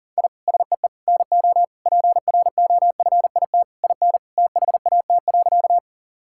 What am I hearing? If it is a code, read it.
Morse code, heard as a keyed tone, about 40 wpm, 700 Hz. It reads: I SEE NO PROFIT IN THAT.
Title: Sentences from Top 1000 Words - Review Part 5 - 40wpm